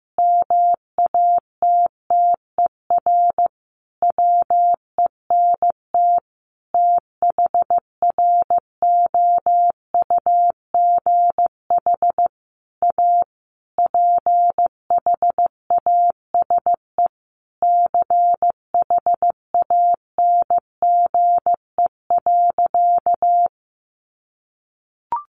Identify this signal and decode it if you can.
Morse code, heard as a keyed tone, about 15 wpm, 700 Hz. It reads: MATTER WENT THROUGH A PHASE CHANGE.
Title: Sentences from Top 1200 Words 15wpm